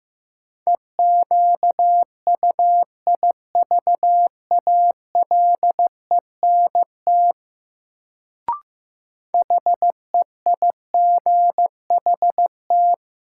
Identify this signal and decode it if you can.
Morse code, heard as a keyed tone, about 15 wpm, 700 Hz. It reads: EQUIVALENT HEIGHT
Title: Sentences from Top 1600 Words - New Word Drill - 15wpm